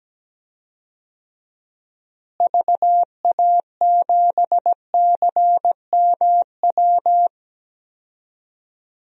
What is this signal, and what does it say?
Morse code, heard as a keyed tone, about 17 wpm, 700 Hz. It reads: VA7CMW
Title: DXCC-100 Call Signs - 6 Characters - Repeated 3x - 17wpm